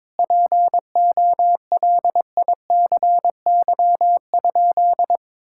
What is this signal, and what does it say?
Morse code, heard as a keyed tone, about 22 wpm, 700 Hz. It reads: POLICY?
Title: Sentences from Top 1000 Words - Review Part 5 - 22wpm